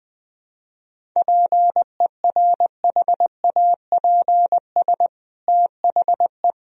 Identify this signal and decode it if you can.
Morse code, heard as a keyed tone, about 20 wpm, 700 Hz. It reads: PERHAPS THE
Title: Sentences from Top 1000 Words - Review Part 1 - 20wpm